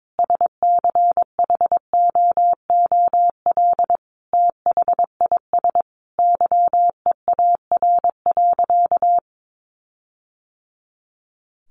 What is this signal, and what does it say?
Morse code, heard as a keyed tone, about 22 wpm, 700 Hz. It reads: SCHOOL THIS YEAR.
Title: Sentences from Top 1400 Words 22wpm